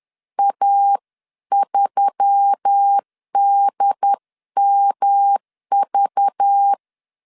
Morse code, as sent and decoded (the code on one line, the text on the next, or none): .- ...-- -.. -- ...-
A3DMV